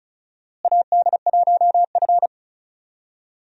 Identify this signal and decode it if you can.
Morse code, heard as a keyed tone, about 36 wpm, 700 Hz. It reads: AD1F
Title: USA Call Signs - 4 Characters - Repeated 3x - 35wpm